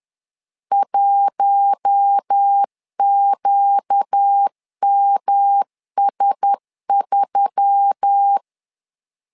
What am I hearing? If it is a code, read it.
Morse code, heard as a keyed tone, about 10 wpm, 800 Hz. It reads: 1QMS3
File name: LEZIONI / 4 Q1S3 10wpm 2427